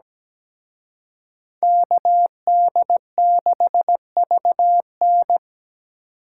Morse code, heard as a keyed tone, about 20 wpm, 700 Hz. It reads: KD6VN
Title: Speed Racing - USA Call Signs 17wpm